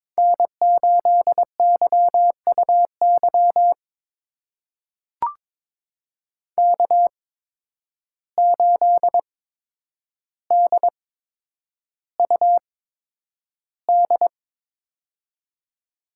Morse code, heard as a keyed tone, about 22 wpm, spaced as for 9 wpm, 700 Hz. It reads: N8YUY K8DUD
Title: USA Call Signs - 5 Characters - 8x Character Spacing - 22wpm